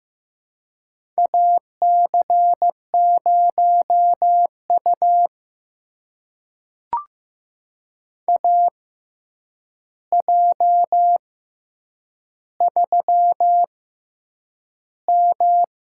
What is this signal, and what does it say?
Morse code, heard as a keyed tone, about 15 wpm, spaced as for 8 wpm, 700 Hz. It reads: AC0U AJ3M
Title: USA Call Signs - 4 Characters - 6x Character Spacing - 15wpm